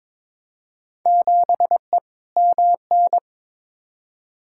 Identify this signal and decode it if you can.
Morse code, heard as a keyed tone, about 21 wpm, 700 Hz. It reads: 7E MN
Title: Field Day Exchanges Encore - 22wpm